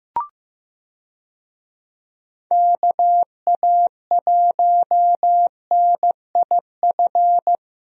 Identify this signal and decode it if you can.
Morse code, heard as a keyed tone, about 15 wpm, 700 Hz. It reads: KA1NIF
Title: USA Call Signs - 6 Characters - Repeated 3x - 15wpm